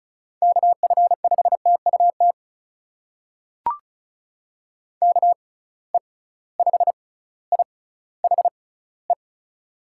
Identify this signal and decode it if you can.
Morse code, heard as a keyed tone, about 36 wpm, spaced as for 18 wpm, 700 Hz. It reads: KF5TUT KE5IHE
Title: USA Call Signs - 6 Characters - 6x Character Spacing - 35wpm